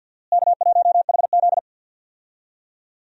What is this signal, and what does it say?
Morse code, heard as a keyed tone, about 52 wpm, 700 Hz. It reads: K1HZ
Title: USA Call Signs - 4 Characters - Repeated 2x - 50wpm